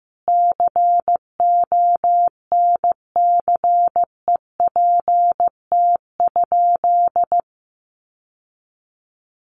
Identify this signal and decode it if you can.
Morse code, heard as a keyed tone, about 15 wpm, 700 Hz. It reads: CONCEPT?
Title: Sentences from Top 1100 Words 15wpm